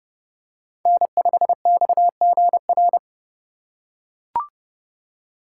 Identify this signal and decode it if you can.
Morse code, heard as a keyed tone, about 30 wpm, 700 Hz. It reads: N5XGR
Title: USA Call Signs - 5 Characters - 3x Character Spacing - 30wpm